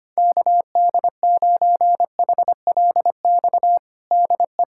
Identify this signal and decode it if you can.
Morse code, heard as a keyed tone, about 25 wpm, 700 Hz. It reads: KD9HLX DE
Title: USA Callsign DE Callsign - 6 Characters - 25wpm